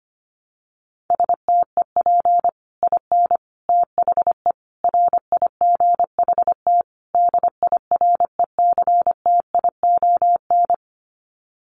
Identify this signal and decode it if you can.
Morse code, heard as a keyed tone, about 25 wpm, 700 Hz. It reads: STEP IN THE RIGHT DIRECTION
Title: Sets of 5 Words Encore 25wpm